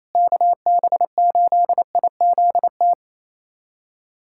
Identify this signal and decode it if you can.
Morse code, heard as a keyed tone, about 28 wpm, 700 Hz. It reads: KB8IZT
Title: USA Callsign DE Callsign 28wpm